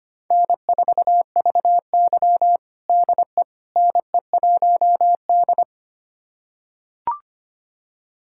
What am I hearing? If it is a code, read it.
Morse code, heard as a keyed tone, about 25 wpm, 700 Hz. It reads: N4VY DE NE1D
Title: USA Callsign DE Callsign - 4 Characters - 25wpm